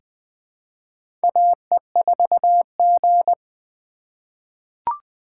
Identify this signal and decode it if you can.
Morse code, heard as a keyed tone, about 20 wpm, 700 Hz. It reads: AE4G
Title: USA Call Signs - 4 Characters - 4x Character Spacing - 20wpm